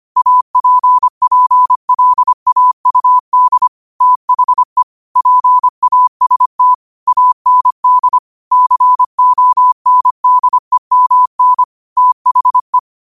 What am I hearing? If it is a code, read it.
Morse code, heard as a keyed tone, about 25 wpm, 1000 Hz. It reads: APPLAUD THE PAST AND CONDEMN THE
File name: qotdfiles / 25wpm